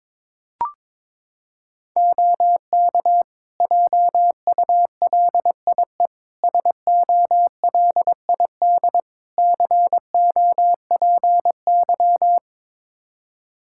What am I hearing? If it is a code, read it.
Morse code, heard as a keyed tone, about 22 wpm, 700 Hz. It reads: OK JULIE SOLID COPY
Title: Practice QSO - Exchange 3 - More Info - 22wpm